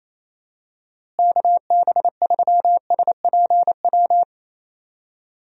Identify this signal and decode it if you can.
Morse code, heard as a keyed tone, about 28 wpm, 700 Hz. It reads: KB3SPW
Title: USA Call Signs - 6 Characters - Repeated 3x - 28wpm